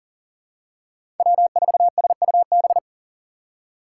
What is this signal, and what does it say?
Morse code, heard as a keyed tone, about 40 wpm, 700 Hz. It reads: W4SUB
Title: USA Call Signs - 5 Characters - 6x Character Spacing - 40wpm